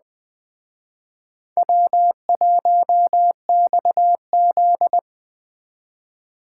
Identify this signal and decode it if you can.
Morse code, heard as a keyed tone, about 21 wpm, 700 Hz. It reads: W1XZ